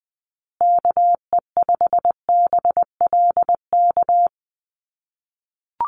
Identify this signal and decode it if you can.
Morse code, heard as a keyed tone, about 20 wpm, 700 Hz. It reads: KE5BLK
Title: USA Call Signs Repeated 2x 20wpm